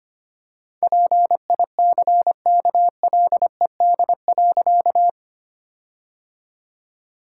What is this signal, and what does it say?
Morse code, heard as a keyed tone, about 25 wpm, 700 Hz. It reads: PICKLED.
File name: practice / 25wpm / All-Age Appropriate Jokes - 25wpm